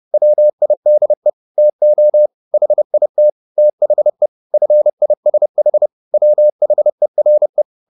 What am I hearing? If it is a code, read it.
Morse code, heard as a keyed tone, about 30 wpm, 600 Hz. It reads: WIDE TO HIT THE FISH WHERE